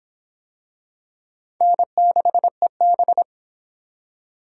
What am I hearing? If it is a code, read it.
Morse code, heard as a keyed tone, about 25 wpm, 700 Hz. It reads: N6EB